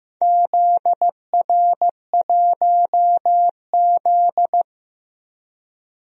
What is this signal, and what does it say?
Morse code, heard as a keyed tone, about 15 wpm, 700 Hz. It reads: ZR1Z
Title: DXCC-100 Call Signs - 4 Characters - Repeated 3x - 15wpm